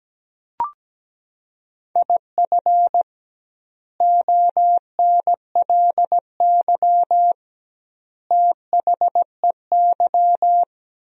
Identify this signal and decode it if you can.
Morse code, heard as a keyed tone, about 17 wpm, 700 Hz. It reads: IF ONLY THEY
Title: 2x Word-Spacing - Sentences from Top 500 Words - 17wpm